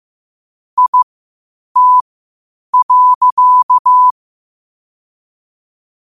Morse code, heard as a keyed tone, about 15 wpm, spaced as for 8 wpm, 1000 Hz. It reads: IT.
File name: qotdfiles / 8wpm